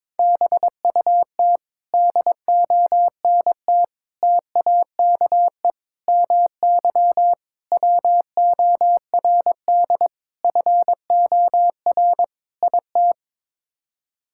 Morse code, heard as a keyed tone, about 22 wpm, 700 Hz. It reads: BUT DONT TAKE MY WORD FOR IT
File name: practice / 22wpm / Sets of 7 Words - 22wpm